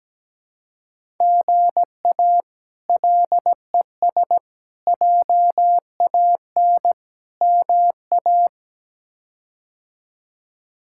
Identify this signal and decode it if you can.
Morse code, heard as a keyed tone, about 17 wpm, 700 Hz. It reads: GA LES JAN MA
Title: K1USN SST QSO Practice - 17wpm